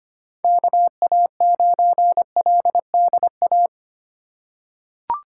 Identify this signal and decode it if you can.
Morse code, heard as a keyed tone, about 25 wpm, 700 Hz. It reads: KA9LDA
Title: USA Call Signs - 4x Character Spacing - 25wpm